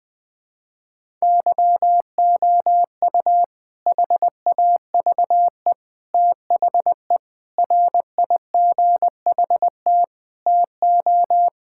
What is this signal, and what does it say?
Morse code, heard as a keyed tone, about 20 wpm, 700 Hz. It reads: YOU HAVE THE RIGHT TO
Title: Sentences with 4-Word Phrases - Spoken Beginning - 20wpm